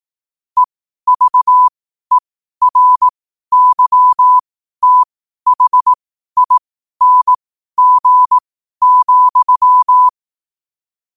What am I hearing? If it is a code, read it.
Morse code, heard as a keyed tone, about 18 wpm, spaced as for 12 wpm, 1000 Hz. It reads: EVERYTHING,
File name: qotdfiles / 12wpm